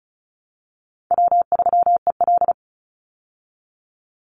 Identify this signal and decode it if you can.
Morse code, heard as a keyed tone, about 34 wpm, 700 Hz. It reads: W3EL
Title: Call Signs Repeated 2x - 35wpm